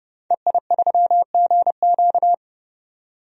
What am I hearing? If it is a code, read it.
Morse code, heard as a keyed tone, about 30 wpm, 690 Hz. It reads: EI3GQ
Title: DXCC-100 Call Signs - 5 Characters - 30wpm